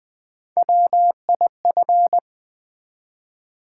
Morse code, heard as a keyed tone, about 20 wpm, 700 Hz. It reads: WIF
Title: Build A Word - 1 Letter At A Time - 4-Letter Words - 20wpm